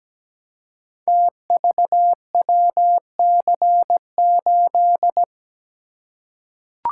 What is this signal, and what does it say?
Morse code, heard as a keyed tone, about 17 wpm, 700 Hz. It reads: TVWC8
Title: Code Groups - 5 Letter:Numbers - 17wpm